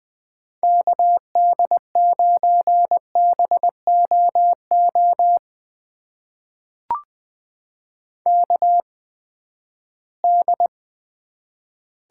Morse code, heard as a keyed tone, about 20 wpm, spaced as for 14 wpm, 700 Hz. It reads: KD9BOO KD
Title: USA Call Signs - 6 Characters - 8x Character Spacing - 20wpm